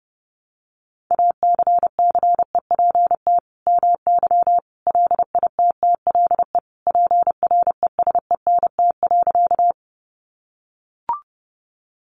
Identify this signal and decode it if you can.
Morse code, heard as a keyed tone, about 30 wpm, 700 Hz. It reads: ACCEPT MY LITTLE PRESENT.
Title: Sentences from Top 1300 Words 30wpm